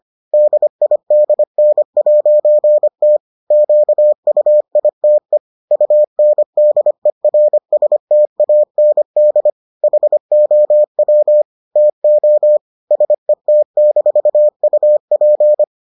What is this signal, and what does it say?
Morse code, heard as a keyed tone, about 25 wpm, 600 Hz. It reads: DIDN'T QUITE UNDERSTAND HOW TO SET-UP